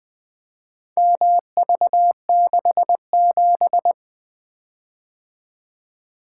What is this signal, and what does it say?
Morse code, heard as a keyed tone, about 24 wpm, 700 Hz. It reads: MV67
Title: Speed-Racing - Four Letter-Number Combinations - 20wpm